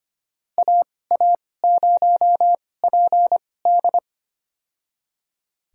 Morse code, heard as a keyed tone, about 25 wpm, spaced as for 18 wpm, 700 Hz. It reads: AA0PD
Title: USA Call Signs - 5 Characters - 2x Character Spacing - 25wpm